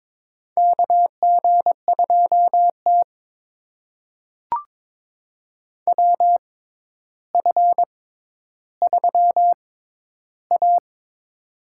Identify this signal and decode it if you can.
Morse code, heard as a keyed tone, about 22 wpm, spaced as for 11 wpm, 700 Hz. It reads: KG2T WF3A
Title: USA Call Signs - 4 Characters - 6x Character Spacing - 22wpm